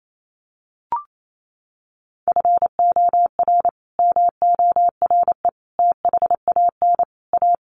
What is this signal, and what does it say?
Morse code, heard as a keyed tone, about 28 wpm, 700 Hz. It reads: FOR MORE THAN A
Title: Sets of 5 Words Encore2 28wpm